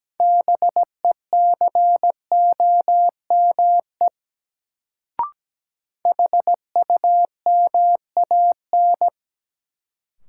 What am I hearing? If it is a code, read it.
Morse code, heard as a keyed tone, about 17 wpm, 700 Hz. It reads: BECOME HUMAN
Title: Sentences from Top 600 Words - New Word Drill - 17wpm